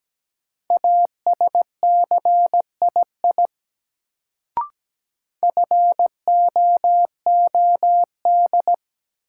Morse code, heard as a keyed tone, about 17 wpm, 700 Hz. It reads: ASCII FOOD
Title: Sets of 5 Words - New Word Drill - 17wpm